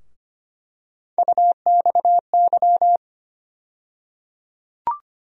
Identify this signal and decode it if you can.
Morse code, heard as a keyed tone, about 25 wpm, 700 Hz. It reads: UXY
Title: USA Call Sign Postfixes - 3 Letters 25wpm